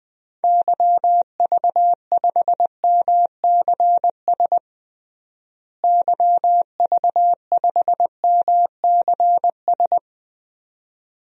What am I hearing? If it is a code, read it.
Morse code, heard as a keyed tone, about 20 wpm, 700 Hz. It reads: YV5MCS YV5MCS
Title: DXCC-100 Call Signs - 6 Characters - Repeated 2x - 20wpm